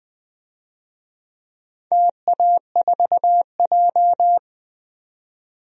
Morse code, heard as a keyed tone, about 20 wpm, 700 Hz. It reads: TA4J
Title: DXCC-100 Call Signs - 4 Characters - 20wpm